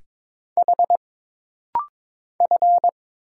Morse code, heard as a keyed tone, about 22 wpm, 700 Hz. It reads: H F
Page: This rapid-fire timing is geared to push your ICR (Instant Character Recognition) skills.